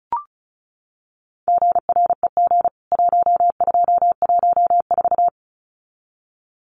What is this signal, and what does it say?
Morse code, heard as a keyed tone, about 34 wpm, 700 Hz. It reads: GREG 1214
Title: CWT Exchange 35wpm